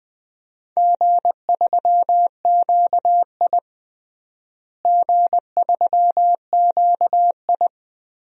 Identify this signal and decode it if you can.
Morse code, heard as a keyed tone, about 20 wpm, 700 Hz. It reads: G3QI G3QI
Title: DXCC-100 Call Signs - 4 Characters - Repeated 3x - 20wpm